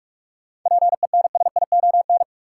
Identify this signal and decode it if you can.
Morse code, heard as a keyed tone, about 47 wpm, 700 Hz. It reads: PENSION
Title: Seven Letter Words - 45wpm